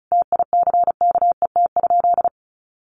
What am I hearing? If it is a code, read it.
Morse code, heard as a keyed tone, about 34 wpm, 700 Hz. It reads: TICKET?
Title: Sentences from Top 1400 Words 35wpm